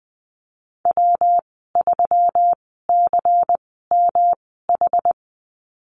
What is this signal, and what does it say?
Morse code, heard as a keyed tone, about 20 wpm, spaced as for 15 wpm, 700 Hz. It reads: W3CMH
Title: 128 Lesson 21g - Number 6 - Call Signs 20wpm